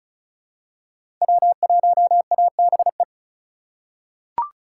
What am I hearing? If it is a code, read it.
Morse code, heard as a keyed tone, about 34 wpm, 700 Hz. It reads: W1ABE
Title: USA Call Signs - 5 Characters - 3x Character Spacing - 35wpm